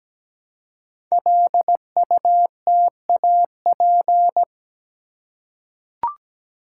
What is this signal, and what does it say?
Morse code, heard as a keyed tone, about 17 wpm, 700 Hz. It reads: LUTAP
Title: Code Groups - 5 Letters - 17wpm